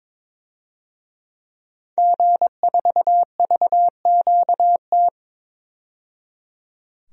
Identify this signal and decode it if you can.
Morse code, heard as a keyed tone, about 22 wpm, 700 Hz. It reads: G4VQT
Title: DXCC-100 Call Signs - 5 Characters - 22wpm